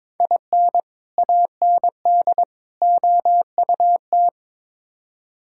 Morse code, heard as a keyed tone, about 22 wpm, 700 Hz. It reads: IN AND OUT
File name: practice / 22wpm / Build A Phrase - 5-Word Phrases - 22wpm